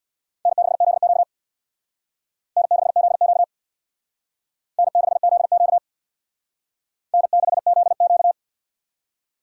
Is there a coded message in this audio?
Yes: Morse code from 0.5 to 8.3 s, heard as a keyed tone, about 65 wpm, 700 Hz. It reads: N6ZQ N6ZQ N6ZQ N6ZQ